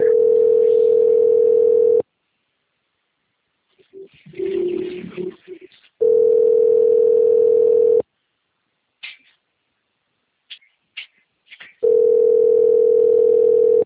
Ringback tone (North America), 440 + 480 Hz.